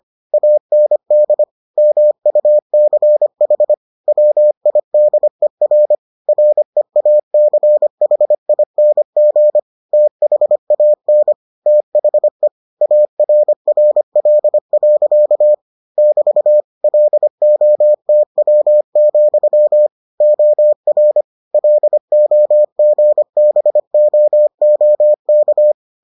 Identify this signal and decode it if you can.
Morse code, heard as a keyed tone, about 25 wpm, 600 Hz. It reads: AND MUCH WIDER REACHING THAN THE ARRL. <BT> LOTW, OR LOGBOOK